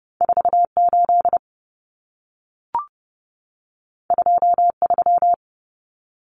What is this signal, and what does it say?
Morse code, heard as a keyed tone, about 30 wpm, 700 Hz. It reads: 48 23
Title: Common Cut Numbers - 2-Digit 30wpm